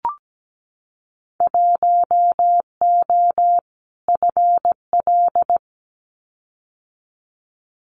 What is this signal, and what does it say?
Morse code, heard as a keyed tone, about 17 wpm, 700 Hz. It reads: 1O FL
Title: Winter Field Day Exchanges 17wpm